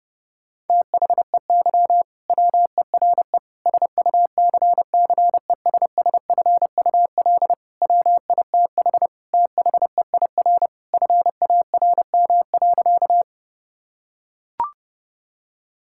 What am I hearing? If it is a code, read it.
Morse code, heard as a keyed tone, about 32 wpm, 700 Hz. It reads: THEY WERE SUCCESSFUL WITH THEIR FARM.
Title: Sentences from Top 1000 Words - Review Part 6 - 30wpm